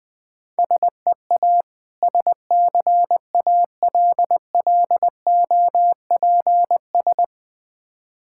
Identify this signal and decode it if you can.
Morse code, heard as a keyed tone, about 20 wpm, 690 Hz. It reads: SEA SCALLOPS
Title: Difficult Words with Hints - 20wpm